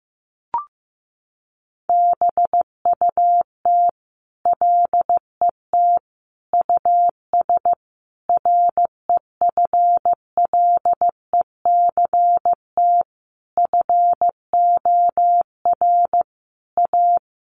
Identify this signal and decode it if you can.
Morse code, heard as a keyed tone, about 15 wpm, 700 Hz. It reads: BUT LET US REFLECT FOR A
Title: Sentences from Top 1500 Words 15wpm